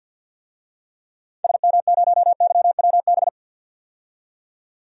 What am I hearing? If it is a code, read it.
Morse code, heard as a keyed tone, about 47 wpm, 700 Hz. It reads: SM0YWB
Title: DXCC-100 Call Signs - 6 Characters - 50wpm